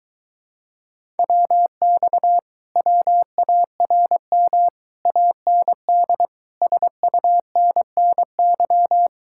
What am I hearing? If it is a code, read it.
Morse code, heard as a keyed tone, about 22 wpm, 700 Hz. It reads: WX WARM AND SUNNY